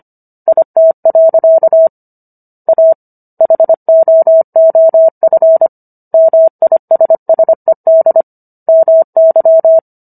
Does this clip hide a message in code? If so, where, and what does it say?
Yes: Morse code from 0.5 to 9.8 s, heard as a keyed tone, about 25 wpm, 650 Hz. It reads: IT. A HOOF MISSED MY